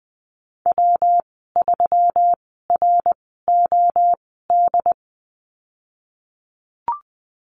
Morse code, heard as a keyed tone, about 20 wpm, spaced as for 15 wpm, 700 Hz. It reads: W3ROD
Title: 122 Lesson 20g - Number 3 - Call Signs 20wpm